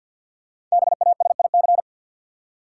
Morse code, heard as a keyed tone, about 53 wpm, 700 Hz. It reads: BASIC
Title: Five Letter Words - 50wpm